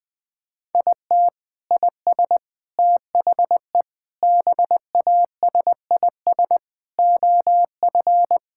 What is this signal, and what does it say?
Morse code, heard as a keyed tone, about 21 wpm, 700 Hz. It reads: IT IS THE BASIS OF
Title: Sentences from Top 1000 Words - Review Part 5 - 20wpm